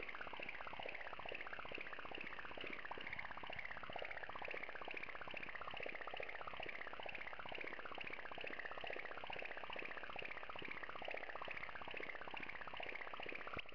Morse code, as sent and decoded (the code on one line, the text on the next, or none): none